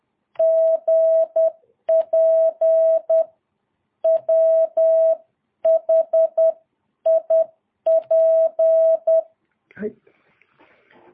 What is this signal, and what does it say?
Morse code, heard as a keyed tone, about 10 wpm, 660 Hz. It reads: GP WHIP
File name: jh8cbh / 2022cw